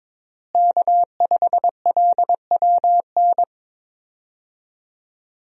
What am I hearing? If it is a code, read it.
Morse code, heard as a keyed tone, about 22 wpm, 700 Hz. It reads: K5LWN
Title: USA Call Signs - 5 Characters - Repeated 3x - 22wpm